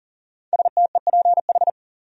Morse code, heard as a keyed tone, about 40 wpm, 700 Hz. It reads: STEPH
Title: USA Common Names - 40wpm